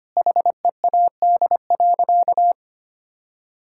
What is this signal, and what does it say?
Morse code, heard as a keyed tone, about 25 wpm, 700 Hz. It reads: HEAD.
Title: Sentences from Top 1000 Words - Review Part 6 - 25wpm